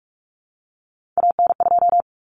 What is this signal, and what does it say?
Morse code, heard as a keyed tone, about 44 wpm, 700 Hz. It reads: AN2
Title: Common Cut Numbers - 3-Digit 45wpm